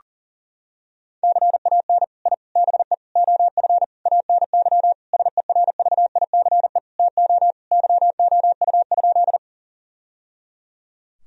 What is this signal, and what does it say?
Morse code, heard as a keyed tone, about 37 wpm, 700 Hz. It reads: CAN I BE OF ANY SERVICE TO YOU?
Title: Sentences from Top 1000 Words - Review Part 4 - 40wpm